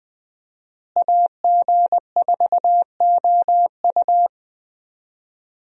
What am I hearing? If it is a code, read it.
Morse code, heard as a keyed tone, about 21 wpm, 700 Hz. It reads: AG4OU
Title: USA Call Signs - 5 Characters - Repeated 3x - 20wpm